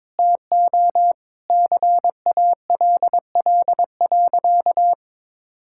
Morse code, heard as a keyed tone, about 22 wpm, 700 Hz. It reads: TO CALL.
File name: practice / 22wpm / Sentences with 4-Word Phrases - 22wpm